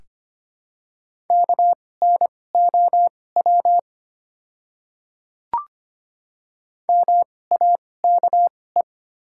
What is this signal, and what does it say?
Morse code, heard as a keyed tone, about 25 wpm, spaced as for 18 wpm, 700 Hz. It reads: KNOW MAKE
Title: 2x Character Spacing - Top 100 Words - 25wpm